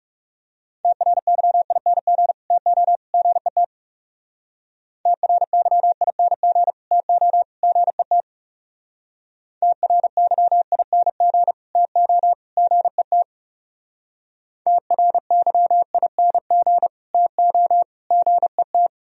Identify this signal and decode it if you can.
Morse code, heard as a keyed tone, about 38 wpm, 700 Hz. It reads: TRYING TO GET TRYING TO GET TRYING TO GET TRYING TO GET